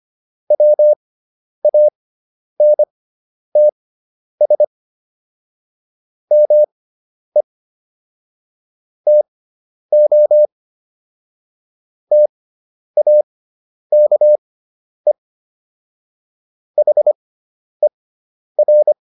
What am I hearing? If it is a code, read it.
Morse code, heard as a keyed tone, about 25 wpm, spaced as for 10 wpm, 600 Hz. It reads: WANTS ME TO TAKE HER